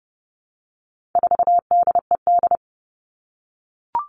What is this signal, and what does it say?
Morse code, heard as a keyed tone, about 30 wpm, 700 Hz. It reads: 4DED
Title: All Cut Numbers - 4-Digit 30wpm